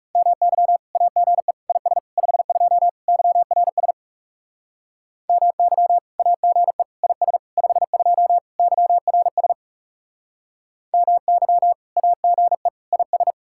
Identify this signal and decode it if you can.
Morse code, heard as a keyed tone, about 40 wpm, 700 Hz. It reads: MY AGE IS 52 YRS MY AGE IS 52 YRS MY AGE IS